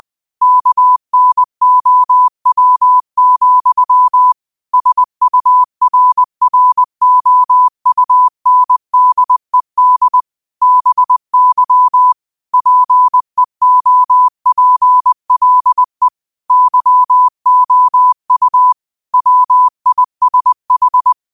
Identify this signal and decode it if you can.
Morse code, heard as a keyed tone, about 20 wpm, 1000 Hz. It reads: KNOW, SURROUNDED BY PEOPLE YOU WISH
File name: 20wpm